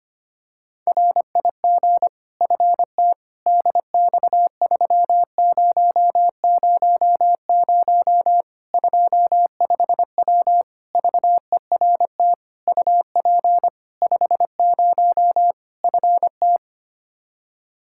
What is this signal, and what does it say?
Morse code, heard as a keyed tone, about 25 wpm, 700 Hz. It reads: RIG FT DX3000 25W VERT UP 50 FT
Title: Practice QSO - Exchange 2 - Info - 25wpm